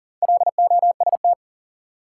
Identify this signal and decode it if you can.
Morse code, heard as a keyed tone, about 37 wpm, 700 Hz. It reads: LOST
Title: Sentences from Top 1400 Words - New Word Drill - 40wpm